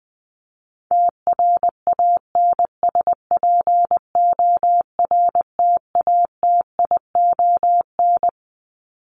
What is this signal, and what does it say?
Morse code, heard as a keyed tone, about 19 wpm, 700 Hz. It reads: TRANSPORTATION
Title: Top 1501-1750 Words 20wpm